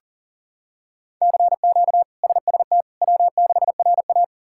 Morse code, heard as a keyed tone, about 40 wpm, 700 Hz. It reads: CQ SST W6RA
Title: K1USN SST QSO Practice - 40wpm